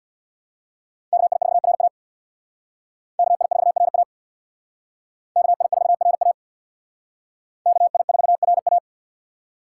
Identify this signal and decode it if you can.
Morse code, heard as a keyed tone, about 61 wpm, 690 Hz. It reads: KI4RA KI4RA KI4RA KI4RA
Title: Speed Racing - USA 2x2 Call Signs 50wpm